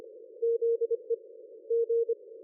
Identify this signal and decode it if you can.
Morse code, heard as a keyed tone, about 25 wpm, 460 Hz. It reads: ZE G
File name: dlazaawansowanych / nagraniazpasma / ExpertSDR2 line out 23.01.2020 14.04.01